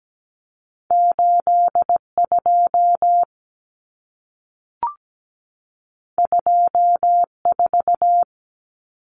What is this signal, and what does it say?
Morse code, heard as a keyed tone, about 17 wpm, 700 Hz. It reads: 82 24
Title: Common Cut Numbers - 2-Digit 17wpm